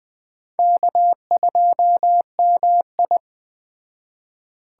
Morse code, heard as a keyed tone, about 20 wpm, 700 Hz. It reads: K2MI